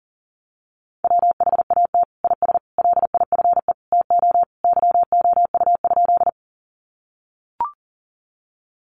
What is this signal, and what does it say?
Morse code, heard as a keyed tone, about 40 wpm, 700 Hz. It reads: WHAT IS LIFE TO YOU?
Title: Sentences from Top 300 Words 40wpm